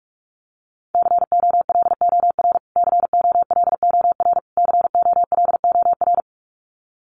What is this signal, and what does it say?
Morse code, heard as a keyed tone, about 44 wpm, 700 Hz. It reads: COLOR COLOR COLOR
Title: Top 500 Words Repeated 45wpm